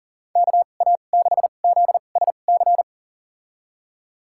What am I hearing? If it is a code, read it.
Morse code, heard as a keyed tone, about 40 wpm, spaced as for 29 wpm, 700 Hz. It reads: KA6ZSC
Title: USA Call Signs - 6 Characters - 2x Character Spacing - 40wpm